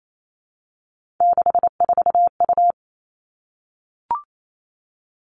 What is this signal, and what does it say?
Morse code, heard as a keyed tone, about 28 wpm, 700 Hz. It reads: 64U
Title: Three Challenging Letter-Number Combinations 28wpm